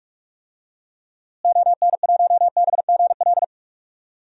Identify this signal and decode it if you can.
Morse code, heard as a keyed tone, about 46 wpm, 700 Hz. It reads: ON1BGL
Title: DXCC-100 Call Signs - 6 Characters - Repeated 2x - 45wpm